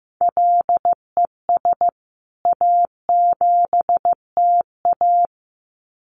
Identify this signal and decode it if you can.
Morse code, heard as a keyed tone, about 15 wpm, 700 Hz. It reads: LES A7TA
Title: CWT Exchange 15wpm